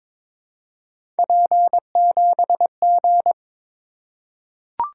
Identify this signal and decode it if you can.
Morse code, heard as a keyed tone, about 21 wpm, 700 Hz. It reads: P7G
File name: practice / 22wpm / Three Letter-Number Combinations - 22wpm